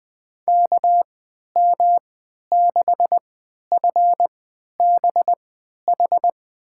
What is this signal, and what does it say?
Morse code, heard as a keyed tone, about 20 wpm, spaced as for 11 wpm, 700 Hz. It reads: KM6FBH
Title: USA Call Signs - 6 Characters - 3x Character Spacing - 20wpm